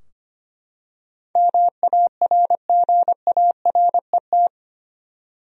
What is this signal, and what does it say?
Morse code, heard as a keyed tone, about 25 wpm, 700 Hz. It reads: MARGARET
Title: USA Common Names - 25wpm